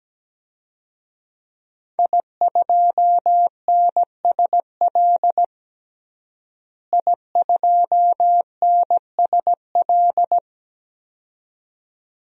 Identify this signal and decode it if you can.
Morse code, heard as a keyed tone, about 17 wpm, 700 Hz. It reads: I2NSL I2NSL
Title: DXCC-100 Call Signs - 5 Characters - Repeated 2x - 17wpm